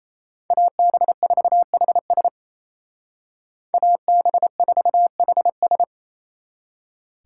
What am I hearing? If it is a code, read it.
Morse code, heard as a keyed tone, about 33 wpm, 700 Hz. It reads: AB4HS AB4HS
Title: Speed Racing - USA 2x2 Call Signs 28wpm